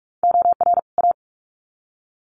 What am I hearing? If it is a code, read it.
Morse code, heard as a keyed tone, about 46 wpm, 700 Hz. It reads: OR A
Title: Sets of 2 Words 45wpm